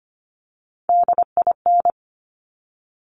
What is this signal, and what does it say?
Morse code, heard as a keyed tone, about 25 wpm, 700 Hz. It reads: DIN
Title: Top Three Letter Combinations in Words 25wpm